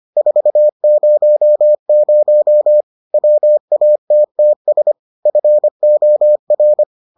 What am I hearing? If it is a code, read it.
Morse code, heard as a keyed tone, about 25 wpm, 600 Hz. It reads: 400 WATTS FOR